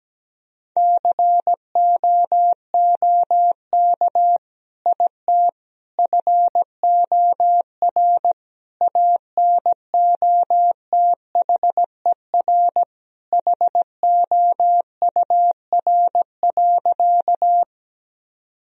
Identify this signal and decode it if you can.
Morse code, heard as a keyed tone, about 17 wpm, 700 Hz. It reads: COOK IT FOR ANOTHER HOUR.
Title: Sentences from Top 1000 Words - Review Part 4 - 17wpm